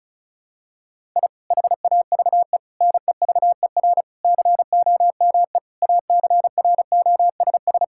Morse code, heard as a keyed tone, about 34 wpm, 700 Hz. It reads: I HAVE NEVER COME ACROSS
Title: Sentences from Top 1000 Words - Review Part 5 - 35wpm